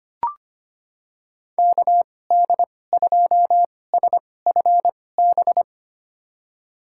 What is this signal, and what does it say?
Morse code, heard as a keyed tone, about 25 wpm, spaced as for 18 wpm, 700 Hz. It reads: KD2SFB
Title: USA Call Signs - 6 Characters - 2x Character Spacing - 25wpm